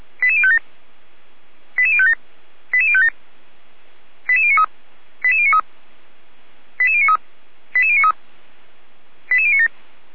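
Five-tone selective call (ZVEI), heard as quick runs of five tones, tones of about 70 ms. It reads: 88057 88057 88057 88093 88093 88093 88093 88097
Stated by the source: ZVEI five-tone selective call 88055, received on SDR.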